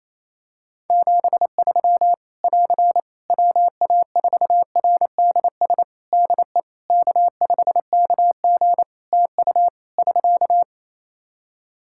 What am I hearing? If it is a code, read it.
Morse code, heard as a keyed tone, about 28 wpm, 700 Hz. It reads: 73 <AR> WA4RDS DE K5KG TU <SK>
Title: Practice QSO - Exchange 4 - Ending - 28wpm